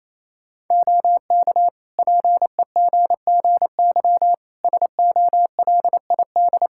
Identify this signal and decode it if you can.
Morse code, heard as a keyed tone, about 28 wpm, 700 Hz. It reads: OK PEGGY SOLID
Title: Practice QSO - Exchange 3 - More Info - 28wpm